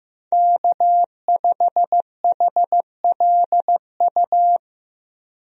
Morse code, heard as a keyed tone, about 15 wpm, 700 Hz. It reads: K5HLU
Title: USA Call Signs - 5 Characters - Repeated 2x - 15wpm